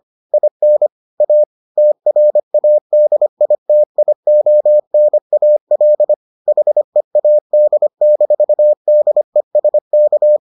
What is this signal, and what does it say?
Morse code, heard as a keyed tone, about 25 wpm, 600 Hz. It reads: IN A TRADITIONAL HEAD-DESK